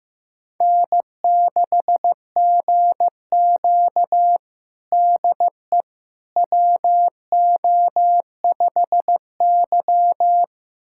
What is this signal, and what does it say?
Morse code, heard as a keyed tone, about 15 wpm, 700 Hz. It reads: N6GQ DE WO5Y
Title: USA Callsign DE Callsign - 4 Characters - 15wpm